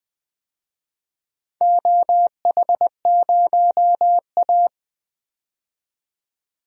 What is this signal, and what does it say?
Morse code, heard as a keyed tone, about 20 wpm, 700 Hz. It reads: OH0A
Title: DXCC-100 Call Signs - 4 Characters - 20wpm